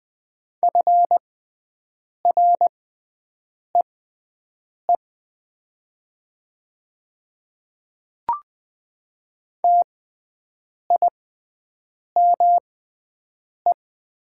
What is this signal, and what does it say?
Morse code, heard as a keyed tone, about 20 wpm, spaced as for 7 wpm, 700 Hz. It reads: FREE TIME
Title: 6x Character Spacing - Top 500 Words - 20wpm